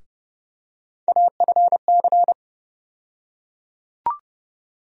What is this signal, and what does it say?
Morse code, heard as a keyed tone, about 30 wpm, 700 Hz. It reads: AFC
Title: USA Call Sign Postfixes - 3 Letters 30wpm